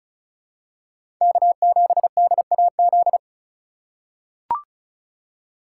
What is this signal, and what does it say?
Morse code, heard as a keyed tone, about 34 wpm, 700 Hz. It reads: K7DAZ
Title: USA Call Signs - 5 Characters - 4x Character Spacing - 35wpm